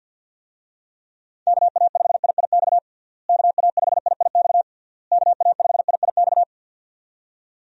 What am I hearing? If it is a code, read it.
Morse code, heard as a keyed tone, about 51 wpm, 700 Hz. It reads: KA5IIX KA5IIX KA5IIX
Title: USA Call Signs - 6 Characters - Repeated 3x - 50wpm